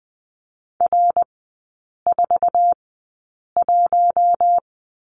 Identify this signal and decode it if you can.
Morse code, heard as a keyed tone, about 20 wpm, spaced as for 8 wpm, 700 Hz. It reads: R41
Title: 191 Lesson 31e - Letter v - Three Characters 20wpm